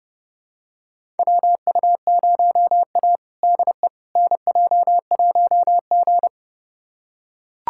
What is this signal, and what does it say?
Morse code, heard as a keyed tone, about 32 wpm, 700 Hz. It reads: WU0A DE NJ1G
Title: USA Callsign DE Callsign - 4 Characters - 30wpm